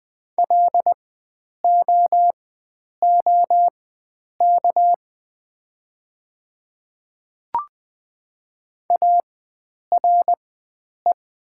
Morse code, heard as a keyed tone, about 20 wpm, spaced as for 9 wpm, 700 Hz. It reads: LOOK ARE
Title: 4x Character Spacing - Top 100 Words - 20wpm